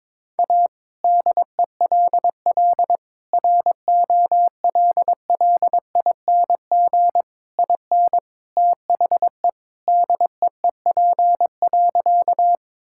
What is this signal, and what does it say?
Morse code, heard as a keyed tone, about 22 wpm, 700 Hz. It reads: A DELL ROLLING IN THE DEEP.
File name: All-Age Appropriate Jokes - 22wpm